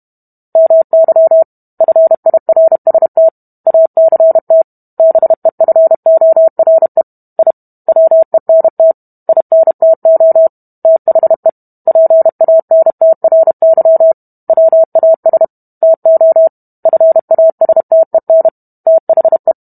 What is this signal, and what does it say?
Morse code, heard as a keyed tone, about 32 wpm, 650 Hz. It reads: MY FIRST ACT BEFORE I WENT INTO THE PANTRY WAS TO FASTEN THE